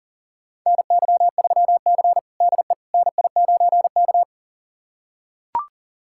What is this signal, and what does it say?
Morse code, heard as a keyed tone, about 40 wpm, 700 Hz. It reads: NY3C DE NI9K
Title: USA Callsign DE Callsign - 4 Characters - 40wpm